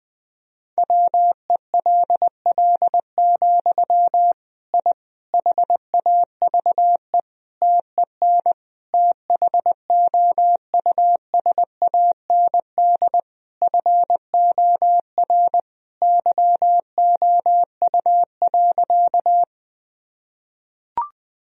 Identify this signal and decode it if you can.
Morse code, heard as a keyed tone, about 21 wpm, 700 Hz. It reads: WELL, I HAVE TEN THOUSAND FOR YOU.
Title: Sentences from Top 1000 Words - Review Part 3 - 20wpm